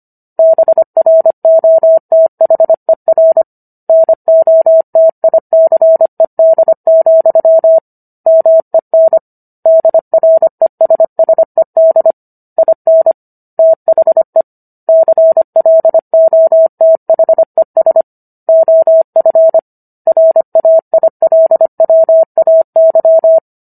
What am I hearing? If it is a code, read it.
Morse code, heard as a keyed tone, about 25 wpm, 650 Hz. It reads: BROTHER NOTICED, MEN DRESSED IN THE CLOTHES OF RAILWAY
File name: Book One / 16-Chapter 16 - 25 WPM